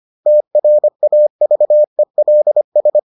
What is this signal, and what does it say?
Morse code, heard as a keyed tone, about 25 wpm, 600 Hz. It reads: TRAVELS